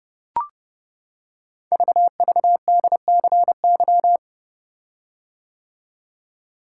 Morse code, heard as a keyed tone, about 30 wpm, 700 Hz. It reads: VVDCY
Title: Code Groups - 5 Letters - 30wpm